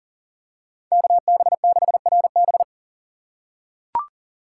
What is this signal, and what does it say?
Morse code, heard as a keyed tone, about 40 wpm, 700 Hz. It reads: KB6RB
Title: USA Call Signs - 5 Characters - 4x Character Spacing - 40wpm